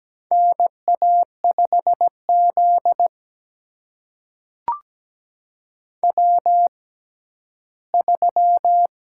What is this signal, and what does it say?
Morse code, heard as a keyed tone, about 17 wpm, spaced as for 11 wpm, 700 Hz. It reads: NA5Z W3
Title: USA Call Signs - 4 Characters - 6x Character Spacing - 17wpm